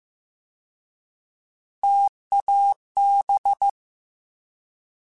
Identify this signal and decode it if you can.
Morse code, heard as a keyed tone, about 15 wpm, 800 Hz. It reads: TAB
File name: T Prefixes / T mp3s